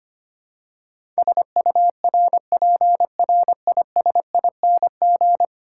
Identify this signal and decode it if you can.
Morse code, heard as a keyed tone, about 25 wpm, 700 Hz. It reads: SURPRISING
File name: Ten Letter Words - 25wpm